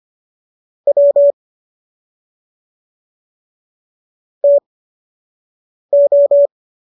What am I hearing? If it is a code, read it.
Morse code, heard as a keyed tone, about 25 wpm, spaced as for 6 wpm, 570 Hz. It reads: W TO